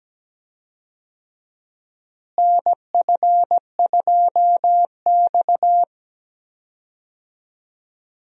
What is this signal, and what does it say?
Morse code, heard as a keyed tone, about 17 wpm, 700 Hz. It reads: NF2X
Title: USA Call Signs - 4 Characters - Repeated 2x - 17wpm